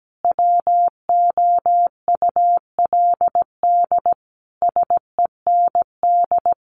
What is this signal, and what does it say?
Morse code, heard as a keyed tone, about 17 wpm, 700 Hz. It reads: WOULD SEND
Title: Sentences from Top 1500 Words 17wpm